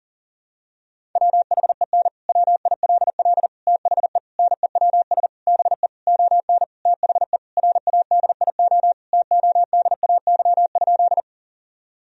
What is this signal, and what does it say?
Morse code, heard as a keyed tone, about 37 wpm, 700 Hz. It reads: WHEN WILL THE NEWS BE ON THE RADIO TODAY?
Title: Sentences from Top 1000 Words - Review Part 3 - 40wpm